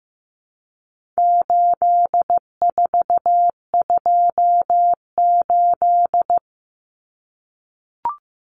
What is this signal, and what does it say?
Morse code, heard as a keyed tone, about 15 wpm, 700 Hz. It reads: 8428
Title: Numbers - 4-Digit 15wpm